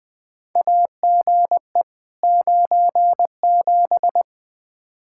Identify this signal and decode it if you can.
Morse code, heard as a keyed tone, about 20 wpm, 700 Hz. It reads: AGE 97
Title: Practice QSO - Exchange 3 - More Info - 20wpm